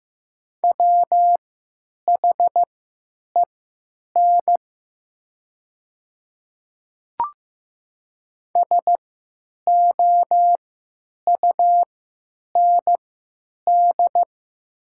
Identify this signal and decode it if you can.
Morse code, heard as a keyed tone, about 15 wpm, spaced as for 9 wpm, 700 Hz. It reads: WHEN SOUND
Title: 3x Character Spacing - Top 200 Words - 15wpm